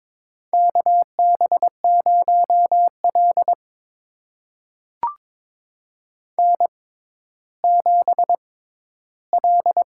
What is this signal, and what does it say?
Morse code, heard as a keyed tone, about 22 wpm, spaced as for 13 wpm, 700 Hz. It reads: KB0L N7L
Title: USA Call Signs - 4 Characters - 6x Character Spacing - 22wpm